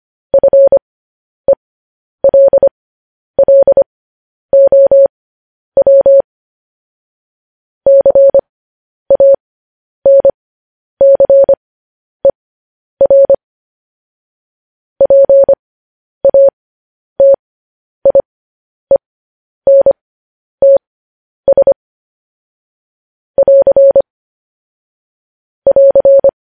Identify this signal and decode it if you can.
Morse code, heard as a keyed tone, about 25 wpm, spaced as for 10 wpm, 550 Hz. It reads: FELLOW CANCER PATIENTS <AR> <AR>
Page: News Headlines in Morse Code at 10 WPM